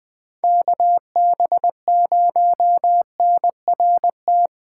Morse code, heard as a keyed tone, about 20 wpm, 700 Hz. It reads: KB0NRT
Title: USA Call Signs - 6 Characters - 8x Character Spacing - 20wpm